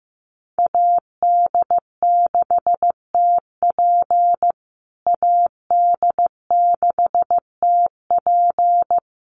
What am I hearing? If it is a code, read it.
Morse code, heard as a keyed tone, about 15 wpm, 700 Hz. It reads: AD6TP AD6TP
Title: USA Call Signs Repeated 3x 15wpm